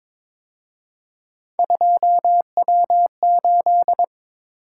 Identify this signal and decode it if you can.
Morse code, heard as a keyed tone, about 26 wpm, 700 Hz. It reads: 2W8
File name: Speed-Racing - Three Letter-Number Combinations - 22wpm